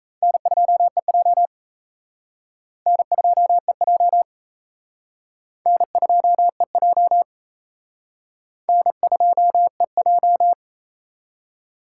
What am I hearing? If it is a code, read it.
Morse code, heard as a keyed tone, about 36 wpm, 700 Hz. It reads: N2EJ N2EJ N2EJ N2EJ